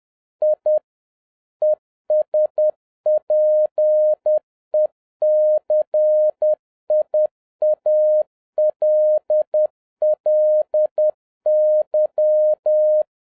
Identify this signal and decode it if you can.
Morse code, heard as a keyed tone, about 10 wpm, 610 Hz. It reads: I ESPECIALLY